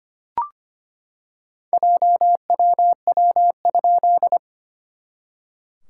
Morse code, heard as a keyed tone, about 25 wpm, 700 Hz. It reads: JWW?
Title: SOTA QSO Practice 25wpm